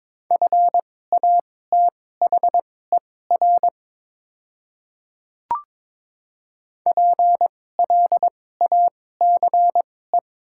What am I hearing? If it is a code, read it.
Morse code, heard as a keyed tone, about 21 wpm, spaced as for 16 wpm, 700 Hz. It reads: FATHER PLACE
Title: 2x Character Spacing - Top 200 Words - 22wpm